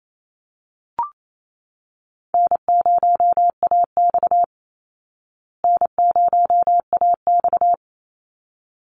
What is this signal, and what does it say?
Morse code, heard as a keyed tone, about 28 wpm, 700 Hz. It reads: N0AX N0AX
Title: Call Signs Repeated 3x - 28wpm